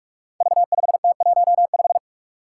Morse code, heard as a keyed tone, about 46 wpm, 700 Hz. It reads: V5T15
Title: Five Challenging Letter-Number Combinations 45wpm